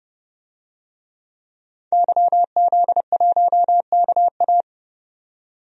Morse code, heard as a keyed tone, about 30 wpm, 700 Hz. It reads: YZ1KA
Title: DXCC-100 Call Signs - 5 Characters - 30wpm